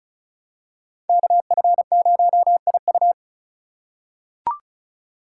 Morse code, heard as a keyed tone, about 36 wpm, 700 Hz. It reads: KF0IU
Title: USA Call Signs - 4x Character Spacing - 35wpm